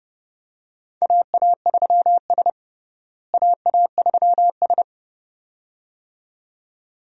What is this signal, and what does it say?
Morse code, heard as a keyed tone, about 32 wpm, 700 Hz. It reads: AA3S AA3S
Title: USA Call Signs - 4 Characters - Repeated 2x - 30wpm